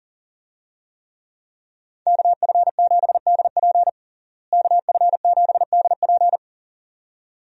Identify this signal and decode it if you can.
Morse code, heard as a keyed tone, about 40 wpm, 700 Hz. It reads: KF7DP KF7DP
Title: USA Call Signs - 5 Characters - Repeated 2x - 40wpm